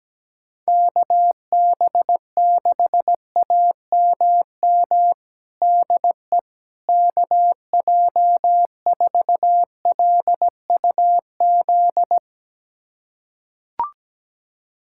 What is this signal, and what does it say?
Morse code, heard as a keyed tone, about 17 wpm, 700 Hz. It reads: KB6AMM DE KJ4LUZ
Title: USA Callsign DE Callsign - 6 Characters - 17wpm